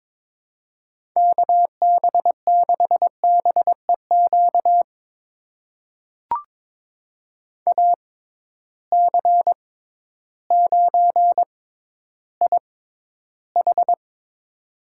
Morse code, heard as a keyed tone, about 23 wpm, spaced as for 12 wpm, 700 Hz. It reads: KB6BEQ AC9IH
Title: USA Call Signs - 6x Character Spacing - 22wpm